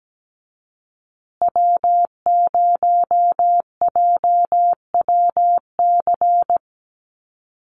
Morse code, heard as a keyed tone, about 17 wpm, 700 Hz. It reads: W0JWC
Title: USA Call Signs Repeated 2x 17wpm